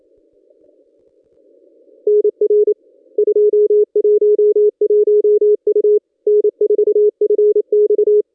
Morse code, heard as a keyed tone, about 27 wpm, 420 Hz. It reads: NR 211 U N4FX
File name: contests / ss-cw-k5tr-op-n5rz -1200